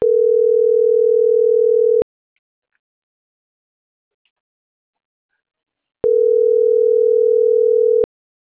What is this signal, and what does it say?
Ringback tone (North America), 440 + 480 Hz.